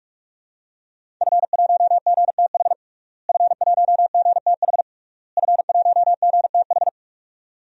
Morse code, heard as a keyed tone, about 46 wpm, 700 Hz. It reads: F1GTH F1GTH F1GTH
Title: DXCC-100 Call Signs - 5 Characters - Repeated 3x - 45wpm